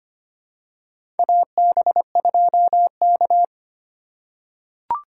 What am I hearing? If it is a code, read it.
Morse code, heard as a keyed tone, about 25 wpm, 700 Hz. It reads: AB2K
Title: USA Call Signs - 4 Characters - 4x Character Spacing - 25wpm